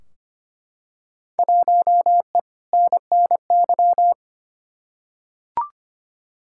Morse code, heard as a keyed tone, about 25 wpm, 700 Hz. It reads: 1E NNY
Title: Field Day Exchanges - 25wpm